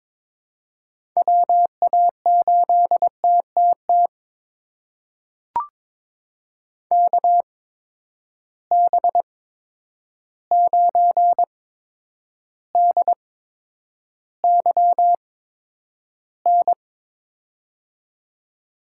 Morse code, heard as a keyed tone, about 22 wpm, spaced as for 9 wpm, 700 Hz. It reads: WA8TTT KB9DYN
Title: USA Call Signs - 6 Characters - 8x Character Spacing - 22wpm